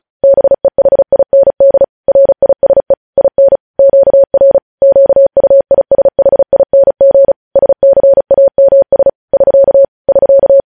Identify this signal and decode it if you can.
Morse code, heard as a keyed tone, about 36 wpm, 550 Hz. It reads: BEHIND RISE IN QR QUISHING SCAMS <SK> <SK>
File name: podcast / News headlines-1x-35x35